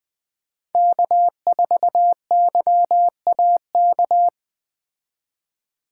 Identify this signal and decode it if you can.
Morse code, heard as a keyed tone, about 20 wpm, 700 Hz. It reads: K4YAK
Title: Speed Racing - USA 1x3 Call Signs 17wpm